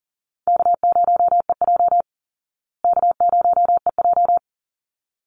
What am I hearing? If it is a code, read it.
Morse code, heard as a keyed tone, about 40 wpm, 700 Hz. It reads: K0EJ K0EJ
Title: Easy Call Signs Repeated 2x - 40wpm